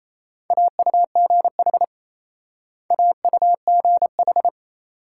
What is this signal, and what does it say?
Morse code, heard as a keyed tone, about 36 wpm, 700 Hz. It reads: AUGH AUGH